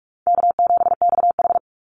Morse code, heard as a keyed tone, about 44 wpm, 700 Hz. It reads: K7XH
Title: Easy Call Signs Repeated 3x - 45wpm